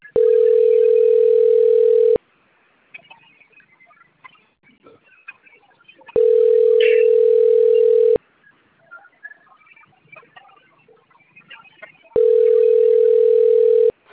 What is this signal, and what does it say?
Ringback tone (North America), 440 + 480 Hz.
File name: out- - - - - .33623